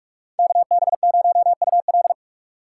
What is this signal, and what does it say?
Morse code, heard as a keyed tone, about 59 wpm, 700 Hz. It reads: KB0UL